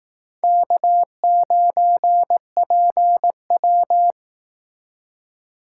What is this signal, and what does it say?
Morse code, heard as a keyed tone, about 19 wpm, 700 Hz. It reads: K9PW
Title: Speed Racing - USA 1x2 Call Signs 15wpm